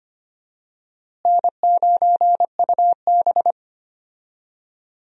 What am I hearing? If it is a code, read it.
Morse code, heard as a keyed tone, about 25 wpm, 700 Hz. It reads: N9UB
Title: USA Call Signs - 4 Characters - Repeated 3x - 25wpm